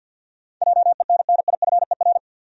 Morse code, heard as a keyed tone, about 50 wpm, 700 Hz. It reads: JENNIFER
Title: USA Common Names - 50wpm